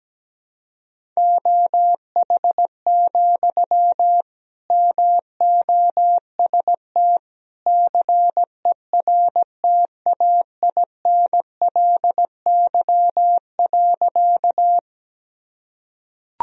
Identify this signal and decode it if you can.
Morse code, heard as a keyed tone, about 17 wpm, 700 Hz. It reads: OH, MOST CERTAINLY.
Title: Sentences from Top 1000 Words - Review Part 1 - 17wpm